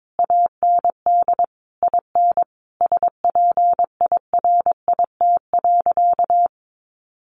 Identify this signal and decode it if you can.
Morse code, heard as a keyed tone, about 22 wpm, 700 Hz. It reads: AND IN SPIRIT.
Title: Sentences from Top 1400 Words 22wpm